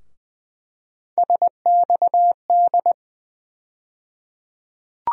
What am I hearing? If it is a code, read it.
Morse code, heard as a keyed tone, about 20 wpm, 700 Hz. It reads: SXD
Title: USA Call Sign Postfixes - 3 Letters 20wpm